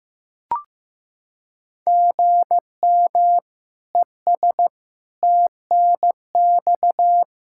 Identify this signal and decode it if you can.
Morse code, heard as a keyed tone, about 16 wpm, 700 Hz. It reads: GM ES TNX
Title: Practice QSO - Exchange 1 - Beginning - 15wpm